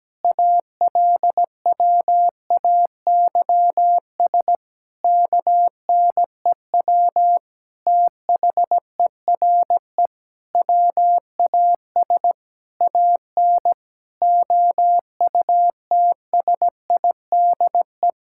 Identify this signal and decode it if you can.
Morse code, heard as a keyed tone, about 17 wpm, 700 Hz. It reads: ALWAYS KNEW THERE WAS AN OUTSIDE
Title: Sentences from Top 1000 Words - Review Part 6 - 17wpm